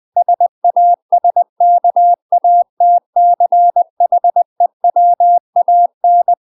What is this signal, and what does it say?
Morse code, heard as a keyed tone, about 20 wpm, 700 Hz. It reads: SASKATCHEWAN
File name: UK MASTER QSOs TRAINING / 20WPM / UK MASTER QSOs 0073